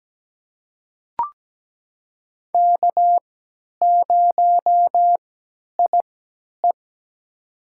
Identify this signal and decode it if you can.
Morse code, heard as a keyed tone, about 17 wpm, spaced as for 10 wpm, 700 Hz. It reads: K0IE
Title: USA Call Signs - 4 Characters - 3x Character Spacing - 17wpm